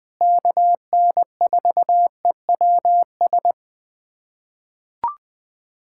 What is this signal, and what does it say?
Morse code, heard as a keyed tone, about 20 wpm, 700 Hz. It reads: KN4EWS
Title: USA Callsign DE Callsign 20wpm